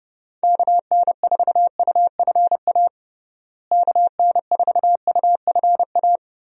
Morse code, heard as a keyed tone, about 30 wpm, 700 Hz. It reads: KN4UFA KN4UFA
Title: USA Call Signs - 6 Characters - Repeated 3x - 30wpm